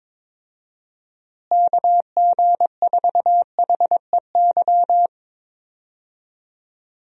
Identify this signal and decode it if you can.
Morse code, heard as a keyed tone, about 22 wpm, 700 Hz. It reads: KG4HEY
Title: USA Call Signs - 6 Characters - Repeated 3x - 22wpm